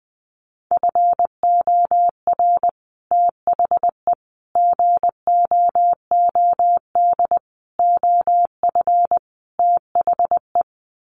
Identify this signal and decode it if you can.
Morse code, heard as a keyed tone, about 20 wpm, 700 Hz. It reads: FOR THE GOOD OF THE
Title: Sets of 5 Words Encore2 20wpm